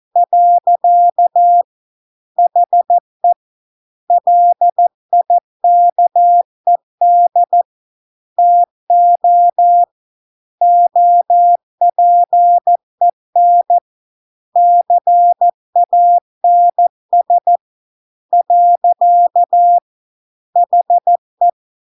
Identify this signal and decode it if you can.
Morse code, heard as a keyed tone, about 14 wpm, 700 Hz. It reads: . HE LIKED TO OPEN CANS . HE